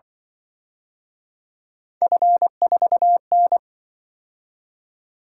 Morse code, heard as a keyed tone, about 24 wpm, 700 Hz. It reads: F4N
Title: Speed-Racing - Three Letter-Number Combinations - 20wpm